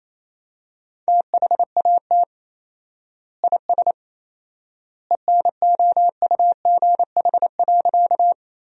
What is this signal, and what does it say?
Morse code, heard as a keyed tone, about 27 wpm, 700 Hz. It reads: THAT IS ENOUGH.
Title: 4x Word-Spacing - Sentences from Top 400 Words - 28wpm